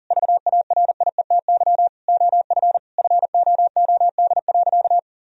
Morse code, heard as a keyed tone, about 40 wpm, 700 Hz. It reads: VARIETY OF FOOD.
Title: Sentences from Top 1000 Words - Review Part 4 - 40wpm